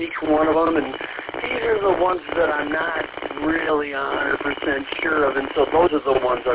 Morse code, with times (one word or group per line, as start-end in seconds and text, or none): none